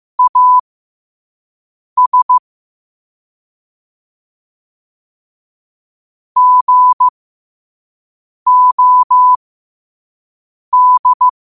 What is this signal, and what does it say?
Morse code, heard as a keyed tone, about 15 wpm, spaced as for 5 wpm, 1000 Hz. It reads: AS GOD
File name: qotdfiles / 5wpm